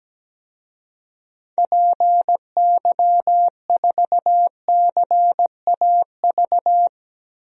Morse code, heard as a keyed tone, about 17 wpm, 700 Hz. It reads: PY4CAV
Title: DXCC-100 Call Signs - 6 Characters - 17wpm